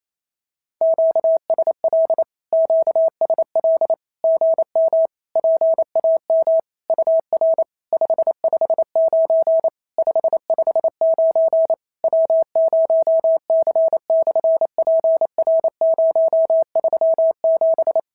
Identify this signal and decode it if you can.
Morse code, heard as a keyed tone, about 28 wpm, 650 Hz. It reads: QSL QSL GM PAM UR 559 559 W0C/PR037
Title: SOTA QSO Practice 28wpm